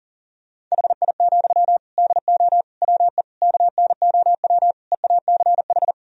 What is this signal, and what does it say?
Morse code, heard as a keyed tone, about 40 wpm, 700 Hz. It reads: HI, DO WE KNOW EACH
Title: Sentences from Top 1000 Words - Review Part 3 - 40wpm